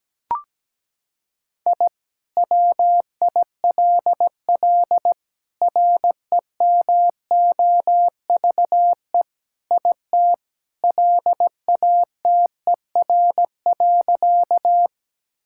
Morse code, heard as a keyed tone, about 17 wpm, 700 Hz. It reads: I WILL REMOVE IT LATER.
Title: Sentences from Top 1000 Words - Review Part 5 - 17wpm